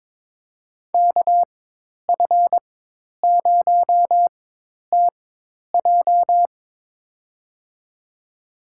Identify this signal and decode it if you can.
Morse code, heard as a keyed tone, about 22 wpm, spaced as for 10 wpm, 700 Hz. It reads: KF0TJ
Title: USA Call Signs - 5 Characters - 4x Character Spacing - 22wpm